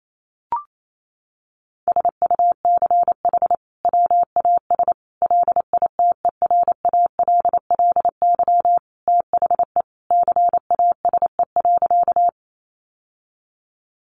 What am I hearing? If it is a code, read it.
Morse code, heard as a keyed tone, about 28 wpm, 700 Hz. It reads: SUCH WAS LITERALLY THE CASE.
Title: Sentences from Top 1400 Words 28wpm